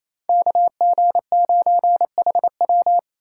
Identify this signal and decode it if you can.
Morse code, heard as a keyed tone, about 36 wpm, 700 Hz. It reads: KG9HW